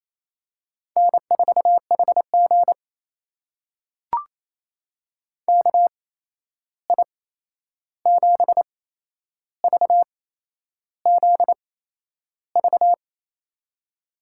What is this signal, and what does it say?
Morse code, heard as a keyed tone, about 28 wpm, spaced as for 11 wpm, 700 Hz. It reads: N4HG KI7VZV
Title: USA Call Signs - 8x Character Spacing - 28wpm